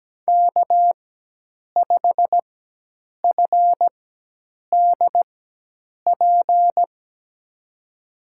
Morse code, heard as a keyed tone, about 17 wpm, spaced as for 8 wpm, 700 Hz. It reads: K5FDP
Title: USA Call Signs - 5 Characters - 4x Character Spacing - 17wpm